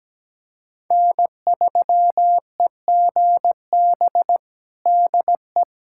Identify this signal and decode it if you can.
Morse code, heard as a keyed tone, about 17 wpm, 700 Hz. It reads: N3EGB DE
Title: USA Callsign DE Callsign - 5 Characters - 17wpm